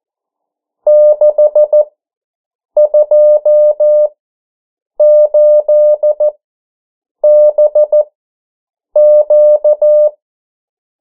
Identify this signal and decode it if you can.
Morse code, heard as a keyed tone, about 14 wpm, spaced as for 7 wpm, 600 Hz. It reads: 628BQ
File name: morse mp3 / 09 urok 8